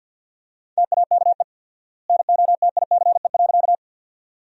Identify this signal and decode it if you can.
Morse code, heard as a keyed tone, about 50 wpm, 700 Hz. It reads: TAKE NOTICE.
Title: 4x Word-Spacing - Sentences from Top 400 Words - 50wpm